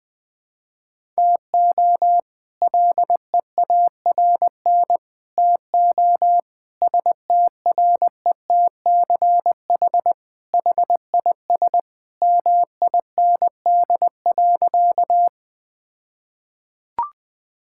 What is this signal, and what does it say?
Morse code, heard as a keyed tone, about 20 wpm, 700 Hz. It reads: TO LEARN TO STRETCH HIS MIND.
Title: All-Age Appropriate Jokes - 20wpm